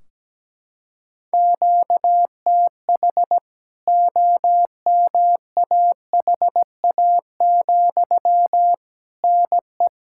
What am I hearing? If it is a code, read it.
Morse code, heard as a keyed tone, about 17 wpm, 700 Hz. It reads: QTH OMAHA, NE
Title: K1USN SST QSO Practice 17wpm